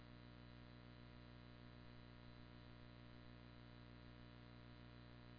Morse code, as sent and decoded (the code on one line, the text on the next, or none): none